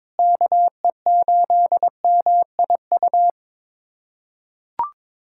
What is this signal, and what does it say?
Morse code, heard as a keyed tone, about 22 wpm, 700 Hz. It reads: KE8MIU
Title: USA Callsign DE Callsign - 6 Characters - 22wpm